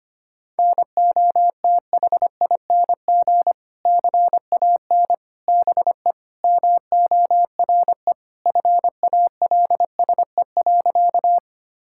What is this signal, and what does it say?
Morse code, heard as a keyed tone, about 25 wpm, 700 Hz. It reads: NOTHING CAN BE MORE FALSE.
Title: Sentences from Top 1600 Words 25wpm